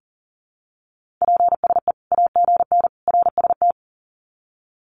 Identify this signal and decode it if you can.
Morse code, heard as a keyed tone, about 40 wpm, 700 Hz. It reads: PSE AGN RST
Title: Common QSO Phrases 40wpm